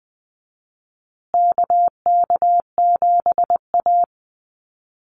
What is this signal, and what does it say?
Morse code, heard as a keyed tone, about 20 wpm, 700 Hz. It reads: KK7A
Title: Call Signs Repeated 2x - 20wpm